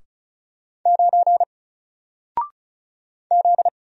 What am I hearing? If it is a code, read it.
Morse code, heard as a keyed tone, about 36 wpm, 700 Hz. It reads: 9 Z
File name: Single-Letter-Number-35wpm